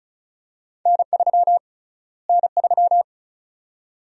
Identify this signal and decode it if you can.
Morse code, heard as a keyed tone, about 36 wpm, 700 Hz. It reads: N3 N3
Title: USA Call Sign Prefixes - Repeated 2x - 35wpm